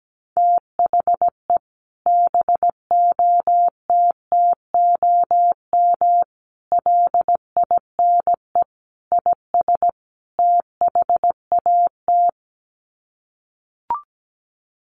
Morse code, heard as a keyed tone, about 17 wpm, 700 Hz. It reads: THE BOTTOM LINE IS THAT
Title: Sets of 5 Words Encore 17wpm